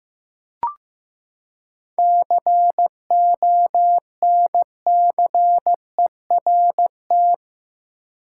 Morse code, heard as a keyed tone, about 16 wpm, 700 Hz. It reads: CONCERT
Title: Seven Letter Words - 15wpm